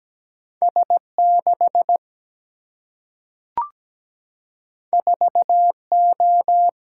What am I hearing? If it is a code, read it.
Morse code, heard as a keyed tone, about 17 wpm, 700 Hz. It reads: S6 4O
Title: Single-Letter Single-Number 17wpm